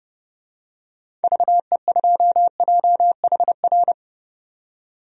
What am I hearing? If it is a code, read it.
Morse code, heard as a keyed tone, about 30 wpm, 700 Hz. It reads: VE2JHR
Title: DXCC-100 Call Signs - 6 Characters - 30wpm